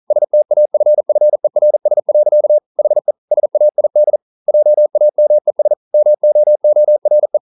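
Morse code, heard as a keyed tone, about 38 wpm, 600 Hz. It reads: STAFFERS. HE SAID JAMES MOORE
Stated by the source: Morse code practice at 41 wpm, from CW med Gnister.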